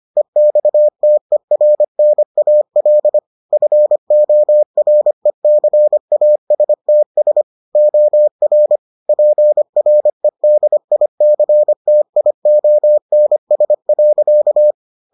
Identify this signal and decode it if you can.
Morse code, heard as a keyed tone, about 25 wpm, 600 Hz. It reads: EXTERNAL FORECASTS OR PREDICTIONS.